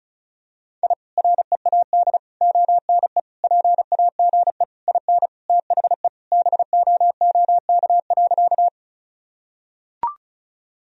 Morse code, heard as a keyed tone, about 34 wpm, 700 Hz. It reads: I READ ONE PAGE IN THE BOOK.
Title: Sentences from Top 1000 Words - Review Part 1 - 35wpm